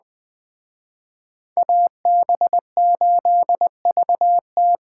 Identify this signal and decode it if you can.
Morse code, heard as a keyed tone, about 20 wpm, 700 Hz. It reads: AB8VT